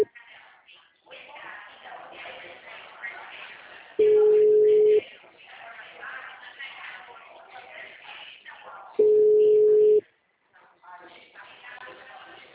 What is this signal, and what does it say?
Ringback tone (Europe, 425 Hz), with speech.